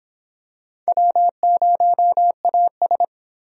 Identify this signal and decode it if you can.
Morse code, heard as a keyed tone, about 23 wpm, 700 Hz. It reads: W0AS